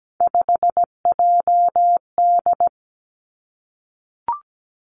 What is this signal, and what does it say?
Morse code, heard as a keyed tone, about 17 wpm, 700 Hz. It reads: 5JD
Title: Three Challenging Letter-Number Combinations 17wpm